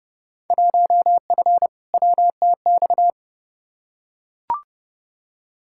Morse code, heard as a keyed tone, about 30 wpm, 700 Hz. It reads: 1F WTX
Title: Field Day Exchanges - 30wpm